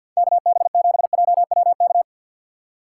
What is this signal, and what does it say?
Morse code, heard as a keyed tone, about 61 wpm, 700 Hz. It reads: KB7JWK
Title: Speed Racing - USA 2x3 Call Signs 50wpm